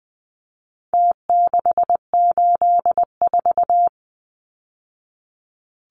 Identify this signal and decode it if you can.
Morse code, heard as a keyed tone, about 20 wpm, 700 Hz. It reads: T684
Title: Numbers - Contest Style - 4-Digit 20wpm